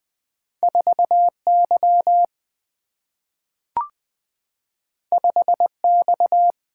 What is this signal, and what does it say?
Morse code, heard as a keyed tone, about 20 wpm, 700 Hz. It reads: 4Y 5X
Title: Single-Letter Single-Number 20wpm